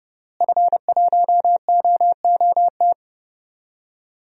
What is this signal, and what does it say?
Morse code, heard as a keyed tone, about 30 wpm, 700 Hz. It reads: F1OOT
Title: DXCC-100 Call Signs - 5 Characters - Repeated 2x - 30wpm